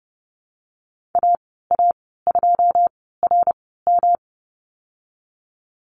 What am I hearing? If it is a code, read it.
Morse code, heard as a keyed tone, about 32 wpm, spaced as for 17 wpm, 700 Hz. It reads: AA2RM
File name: Lesson 19g - Letter w - Call Signs 30wpm